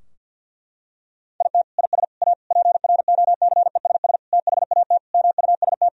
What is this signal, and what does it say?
Morse code, heard as a keyed tone, about 48 wpm, 700 Hz. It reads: IT IS A PROCESS THAT MUST
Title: Sentences from Top 1000 Words - Review Part 3 - 50wpm